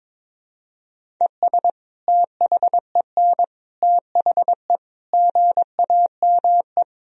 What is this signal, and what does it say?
Morse code, heard as a keyed tone, about 22 wpm, 700 Hz. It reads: ES THEN THE GAME
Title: Sentences from Top 500 Words and Abbreviations 22wpm